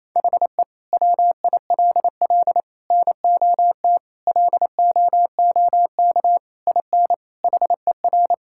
Morse code, heard as a keyed tone, about 28 wpm, 700 Hz. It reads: HE WILL NOT LOOK IN HER
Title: Sentences from Top 1000 Words - Review Part 5 - 28wpm